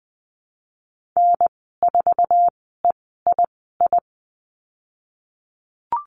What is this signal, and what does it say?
Morse code, heard as a keyed tone, about 20 wpm, spaced as for 15 wpm, 700 Hz. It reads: N4EII
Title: 048 Lesson 9g - Number 4 - Call Signs 20wpm